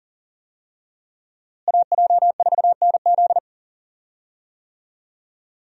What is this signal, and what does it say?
Morse code, heard as a keyed tone, about 37 wpm, 700 Hz. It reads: AJ4NZ
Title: USA Call Signs - 5 Characters - Repeated 2x - 40wpm